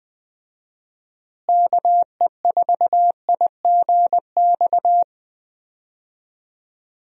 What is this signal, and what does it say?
Morse code, heard as a keyed tone, about 21 wpm, 700 Hz. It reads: KE4IGX
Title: USA Call Signs - 6 Characters - Repeated 3x - 20wpm